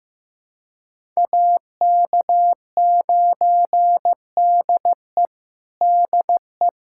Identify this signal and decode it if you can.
Morse code, heard as a keyed tone, about 15 wpm, 700 Hz. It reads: AK9DE DE
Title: USA Callsign DE Callsign - 5 Characters - 15wpm